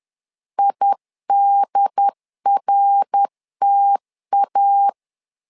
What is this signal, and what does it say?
Morse code, heard as a keyed tone, about 10 wpm, 800 Hz. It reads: IDRTA
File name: LEZIONI / 3 MWRO / 3 MWR 10wpm 1952